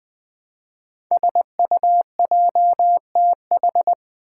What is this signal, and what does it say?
Morse code, heard as a keyed tone, about 20 wpm, 700 Hz. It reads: SUJTH
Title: Code Groups - 5 Letter:Numbers - 20wpm